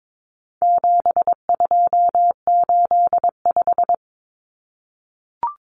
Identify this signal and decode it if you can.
Morse code, heard as a keyed tone, about 22 wpm, 700 Hz. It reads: 7285
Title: Common Cut Numbers - 4-Digit 22wpm